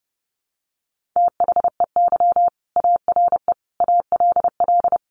Morse code, heard as a keyed tone, about 30 wpm, 700 Hz. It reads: THEY ARE ALL
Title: Additional Easy Phrases 30wpm